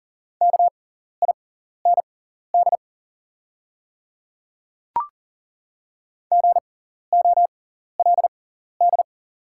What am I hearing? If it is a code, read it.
Morse code, heard as a keyed tone, about 40 wpm, spaced as for 14 wpm, 700 Hz. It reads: KIND GOLD
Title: 6x Character Spacing - Top 500 Words - 40wpm